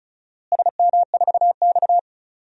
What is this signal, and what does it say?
Morse code, heard as a keyed tone, about 34 wpm, 700 Hz. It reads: SM4X
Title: DXCC-100 Call Signs - 4 Characters - Repeated 3x - 35wpm